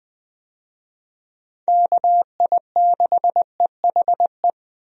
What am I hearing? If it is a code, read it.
Morse code, heard as a keyed tone, about 20 wpm, 700 Hz. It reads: KI6EHE
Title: USA Call Signs - 6 Characters - 20wpm